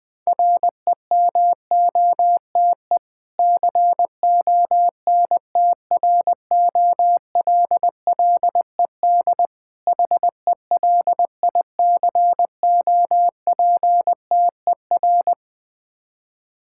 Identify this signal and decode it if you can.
Morse code, heard as a keyed tone, about 21 wpm, 690 Hz. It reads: REMOTE CONTROLLED HELICOPTER
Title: Difficult Words with Hints - 20wpm